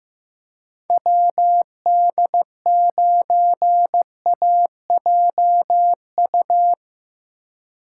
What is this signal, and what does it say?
Morse code, heard as a keyed tone, about 17 wpm, 700 Hz. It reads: WD9AJU
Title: Speed Racing - USA 2x3 Call Signs 15wpm